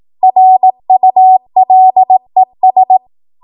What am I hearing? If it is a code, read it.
Morse code, heard as a keyed tone, about 18 wpm, 750 Hz. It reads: RULES
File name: Morse / 18WPM